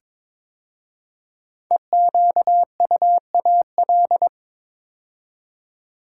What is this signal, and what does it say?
Morse code, heard as a keyed tone, about 22 wpm, 700 Hz. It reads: EQUAL
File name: 5 Letter Words - First 3 Letters Given - 22wpm